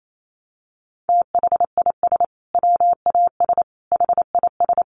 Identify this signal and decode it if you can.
Morse code, heard as a keyed tone, about 28 wpm, 700 Hz. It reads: THIS WAS HIS
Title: Sentences from Top 1200 Words 28wpm